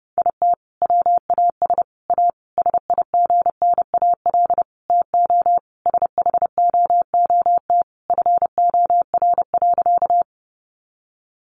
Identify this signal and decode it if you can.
Morse code, heard as a keyed tone, about 30 wpm, 700 Hz. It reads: IT WAS A SIGNAL TO SHOOT FOR.
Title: Sentences from Top 1400 Words 30wpm